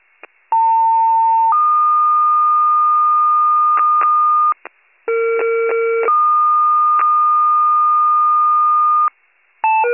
Two-tone page, heard as a one-second tone, then a three-second tone, about 900 then 1220 Hz, about 460 then 1220 Hz. It is heard from 0.5 to 9.1 s.